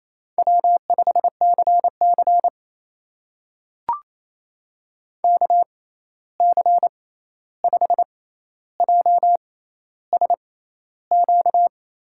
Morse code, heard as a keyed tone, about 28 wpm, spaced as for 13 wpm, 700 Hz. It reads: W5CC KC5JSQ
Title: USA Call Signs - 6x Character Spacing - 28wpm